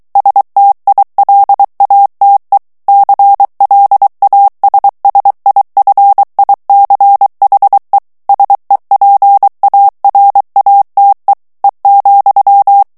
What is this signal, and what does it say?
Morse code, heard as a keyed tone, about 23 wpm, 800 Hz. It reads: STILATE CLASSIFICHE SEPARATE E,
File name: ITALIEN texte clair 2 vitesse 22 mots minute